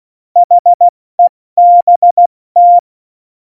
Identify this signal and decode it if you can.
Morse code, heard as a keyed tone, about 16 wpm, 700 Hz. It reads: HEBT
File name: Morsekurs / Witze / Witz 19 Mai, 2020 70 F80